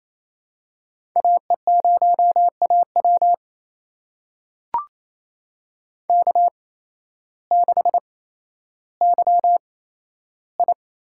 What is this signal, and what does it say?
Morse code, heard as a keyed tone, about 28 wpm, spaced as for 13 wpm, 700 Hz. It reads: AE0AW K6YI
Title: USA Call Signs - 8x Character Spacing - 28wpm